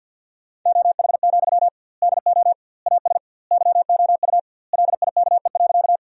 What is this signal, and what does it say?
Morse code, heard as a keyed tone, about 50 wpm, 700 Hz. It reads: OH, DO AS YOU LIKE.
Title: 2x Word-Spacing - Sentences from Top 500 Words - 50wpm